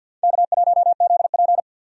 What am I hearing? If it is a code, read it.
Morse code, heard as a keyed tone, about 48 wpm, 700 Hz. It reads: K1ZP
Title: USA Call Signs - 4 Characters - 50wpm